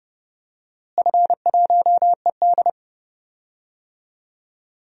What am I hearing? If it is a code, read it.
Morse code, heard as a keyed tone, about 30 wpm, 700 Hz. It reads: F1ED
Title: DXCC-100 Call Signs - 4 Characters - Repeated 3x - 30wpm